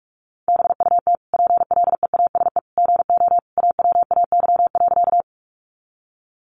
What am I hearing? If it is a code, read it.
Morse code, heard as a keyed tone, about 44 wpm, 700 Hz. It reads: BUT PLEASE GO AWAY.
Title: Sentences from Top 1500 Words 45wpm